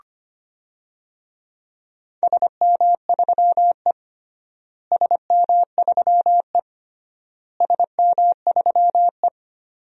Morse code, heard as a keyed tone, about 25 wpm, 700 Hz. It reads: SM3E SM3E SM3E
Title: DXCC-100 Call Signs - 4 Characters - Repeated 3x - 25wpm